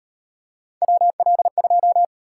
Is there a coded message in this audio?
Yes: Morse code from 0.8 to 2.0 s, heard as a keyed tone, about 36 wpm, 700 Hz. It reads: WL2